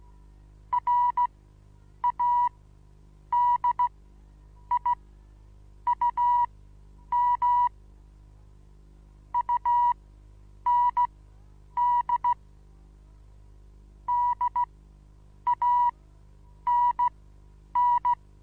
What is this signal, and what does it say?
Morse code, heard as a keyed tone, about 15 wpm, spaced as for 8 wpm, 970 Hz. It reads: RADIUM UND DANN
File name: Onlinelehrgang / morsen / DJ4UF Morsen25